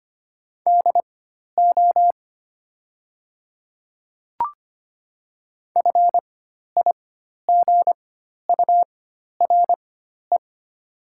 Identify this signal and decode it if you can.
Morse code, heard as a keyed tone, about 25 wpm, spaced as for 12 wpm, 700 Hz. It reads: DO FIGURE
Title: 4x Character Spacing - Top 500 Words - 25wpm